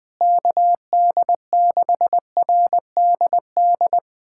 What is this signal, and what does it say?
Morse code, heard as a keyed tone, about 20 wpm, 700 Hz. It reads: KD6RDD
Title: USA Callsign DE Callsign - 6 Characters - 20wpm